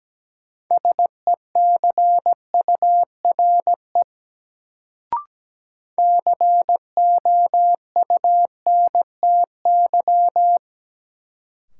Morse code, heard as a keyed tone, about 17 wpm, 700 Hz. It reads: SECURE COUNTY
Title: Sentences from Top 1200 Words - New Word Drill Alt - 17wpm